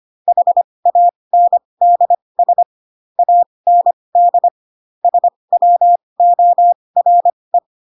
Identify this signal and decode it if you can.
Morse code, heard as a keyed tone, about 25 wpm, spaced as for 20 wpm, 700 Hz. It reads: HANDS AND SWORE
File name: Pinocchio / Pinocchio ch2 15 25 ik1hge